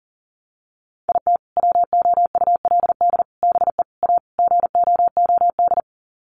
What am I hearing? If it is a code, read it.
Morse code, heard as a keyed tone, about 40 wpm, 700 Hz. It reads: IT WOULD BE A GOOD
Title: Sets of 5 Words Encore2 40wpm